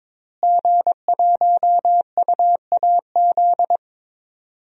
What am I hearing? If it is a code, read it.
Morse code, heard as a keyed tone, about 21 wpm, 700 Hz. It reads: G1UAZ
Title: DXCC-100 Call Signs - 5 Characters - Repeated 2x - 22wpm